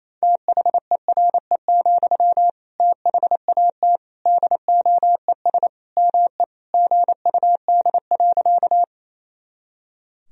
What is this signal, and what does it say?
Morse code, heard as a keyed tone, about 28 wpm, 700 Hz. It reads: THERE, THAT DOES ME GUD.
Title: Sentences from Top 200 Words and Abbreviations 28wpm